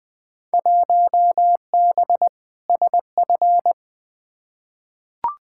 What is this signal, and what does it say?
Morse code, heard as a keyed tone, about 20 wpm, 700 Hz. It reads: 1B SF
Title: Field Day Exchanges Encore - 20wpm